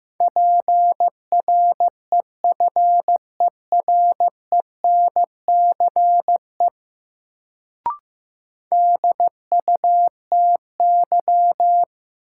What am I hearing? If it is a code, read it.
Morse code, heard as a keyed tone, about 15 wpm, 700 Hz. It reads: PREFERENCE DUTY
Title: Sentences from Top 1500 Words - New Word Drill Alt - 15wpm